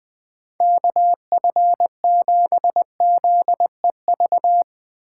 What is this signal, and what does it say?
Morse code, heard as a keyed tone, about 19 wpm, 700 Hz. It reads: KF7ZEV
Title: Speed Racing - USA 2x3 Call Signs 17wpm